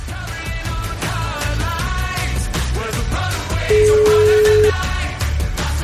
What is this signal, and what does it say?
Ringback tone (Europe, 425 Hz), with noise.